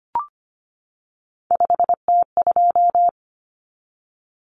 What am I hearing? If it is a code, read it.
Morse code, heard as a keyed tone, about 25 wpm, 700 Hz. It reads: 5T2
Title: Common Cut Numbers - 3-Digit 25wpm